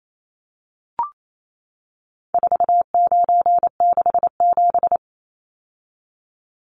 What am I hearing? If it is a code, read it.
Morse code, heard as a keyed tone, about 28 wpm, 700 Hz. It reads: 4967
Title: Numbers - 4-Digit 28wpm